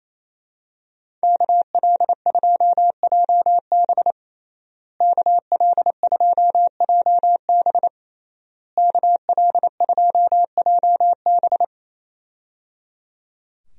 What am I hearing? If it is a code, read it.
Morse code, heard as a keyed tone, about 27 wpm, 700 Hz. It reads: KL2JB KL2JB KL2JB
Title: USA Call Signs - 5 Characters - Repeated 3x - 28wpm